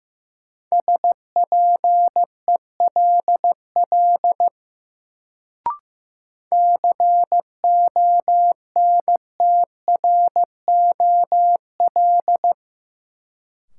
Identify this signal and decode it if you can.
Morse code, heard as a keyed tone, about 15 wpm, 700 Hz. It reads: SPELL CONTROL
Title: Sentences from Top 1200 Words - New Word Drill Alt - 15wpm